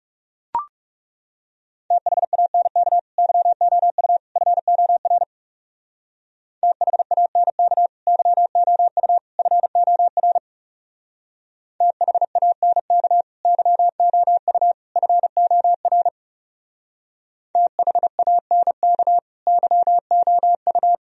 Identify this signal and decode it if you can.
Morse code, heard as a keyed tone, about 38 wpm, 700 Hz. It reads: THANK YOU FOR THANK YOU FOR THANK YOU FOR THANK YOU